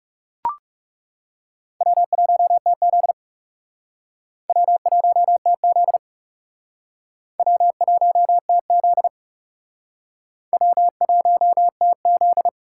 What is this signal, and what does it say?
Morse code, heard as a keyed tone, about 38 wpm, 700 Hz. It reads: W1TZ W1TZ W1TZ W1TZ